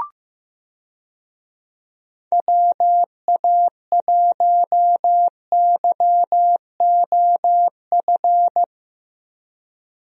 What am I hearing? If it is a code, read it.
Morse code, heard as a keyed tone, about 15 wpm, 700 Hz. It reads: WA1YOF
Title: USA Call Signs - 6 Characters - 15wpm